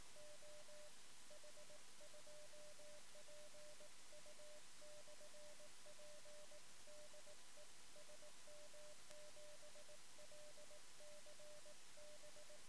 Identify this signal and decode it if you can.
Morse code, heard as a keyed tone, about 18 wpm, 600 Hz. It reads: OH2PU/P DE SM7LCB
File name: sm7lcb / 1 oh2pu-p 13cm